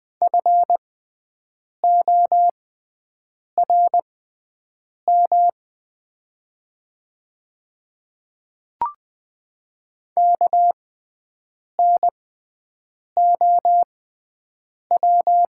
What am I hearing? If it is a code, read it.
Morse code, heard as a keyed tone, about 20 wpm, spaced as for 7 wpm, 700 Hz. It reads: FORM KNOW
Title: 6x Character Spacing - Top 200 Words - 20wpm